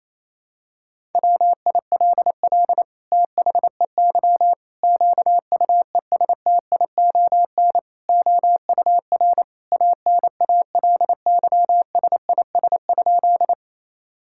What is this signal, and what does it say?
Morse code, heard as a keyed tone, about 28 wpm, 700 Hz. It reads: WILL THEY QUESTION OUR ANALYSIS?
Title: Sentences from Top 1000 Words - Review Part 5 - 28wpm